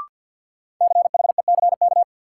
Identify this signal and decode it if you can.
Morse code, heard as a keyed tone, about 46 wpm, 700 Hz. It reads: CHECK
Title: Sentences from Top 1400 Words - New Word Drill Alt - 50wpm